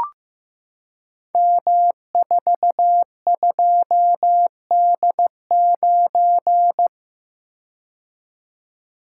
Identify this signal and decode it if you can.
Morse code, heard as a keyed tone, about 16 wpm, 700 Hz. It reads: M42D9
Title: Code Groups - 5 Letter:Numbers - 15wpm